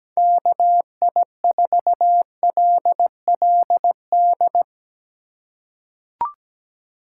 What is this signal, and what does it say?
Morse code, heard as a keyed tone, about 17 wpm, 700 Hz. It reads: KI4LLD
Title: USA Call Signs - 6 Characters - 3x Character Spacing - 17wpm